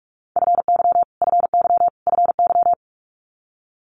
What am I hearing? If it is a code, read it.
Morse code, heard as a keyed tone, about 46 wpm, 700 Hz. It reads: FY FY FY
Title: Multi-letter Phonemes Repeated 3x 45wpm